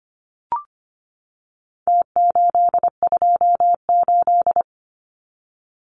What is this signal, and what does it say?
Morse code, heard as a keyed tone, about 25 wpm, 700 Hz. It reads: T828
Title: Numbers - Contest Style - 4-Digit 25wpm